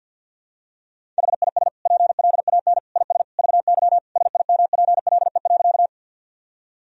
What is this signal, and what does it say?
Morse code, heard as a keyed tone, about 48 wpm, 700 Hz. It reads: HIS PLAN IS VY SIMPLE.
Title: Sentences from Top 500 Words and Abbreviations 50wpm